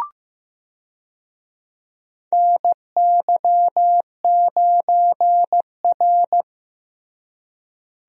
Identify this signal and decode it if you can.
Morse code, heard as a keyed tone, about 15 wpm, 700 Hz. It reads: NY9R
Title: USA Call Signs - 4 Characters - Repeated 3x - 15wpm